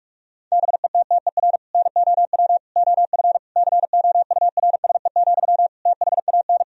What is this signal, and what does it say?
Morse code, heard as a keyed tone, about 41 wpm, 700 Hz. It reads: BETTER NOW OF COURSE, THAN
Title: Sentences from Top 1000 Words - Review Part 3 - 45wpm